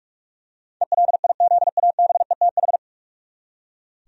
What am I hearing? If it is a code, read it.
Morse code, heard as a keyed tone, about 46 wpm, 700 Hz. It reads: ELIZABETH
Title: USA Common Names - 45wpm